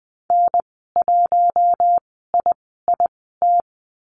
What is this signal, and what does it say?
Morse code, heard as a keyed tone, about 20 wpm, spaced as for 14 wpm, 700 Hz. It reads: N1IIT
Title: 042 Lesson 8g - Number 1 - Call Signs 20wpm